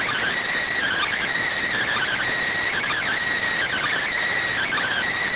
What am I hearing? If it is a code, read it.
Emergency Alert System (SAME) tones (header bursts), heard as harsh buzzing bursts. It is heard from 0.7 to 5.1 s.